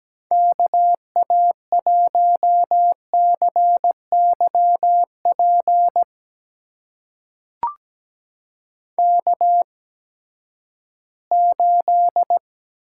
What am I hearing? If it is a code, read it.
Morse code, heard as a keyed tone, about 17 wpm, spaced as for 12 wpm, 700 Hz. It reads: KA1CYP K8
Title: USA Call Signs - 8x Character Spacing - 17wpm